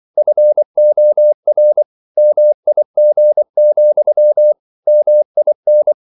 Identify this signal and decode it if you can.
Morse code, heard as a keyed tone, about 24 wpm, 600 Hz. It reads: FOR MIG, MIN
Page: Ferie 24 wpm